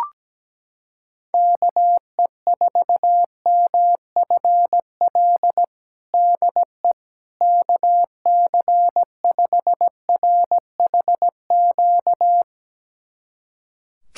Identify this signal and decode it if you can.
Morse code, heard as a keyed tone, about 17 wpm, 700 Hz. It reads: KE4MFL DE KC5RHQ
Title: USA Callsign DE Callsign 17wpm